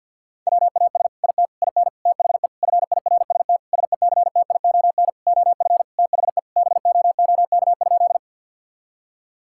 Morse code, heard as a keyed tone, about 51 wpm, 700 Hz. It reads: WAS IT IN THE FIRST SECTION OF THE BOOK?
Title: Sentences from Top 1000 Words - Review Part 5 - 50wpm